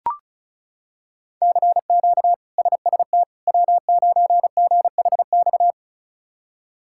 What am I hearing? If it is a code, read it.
Morse code, heard as a keyed tone, about 34 wpm, 700 Hz. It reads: CQ SST W9GHX
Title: K1USN SST QSO Practice - 35wpm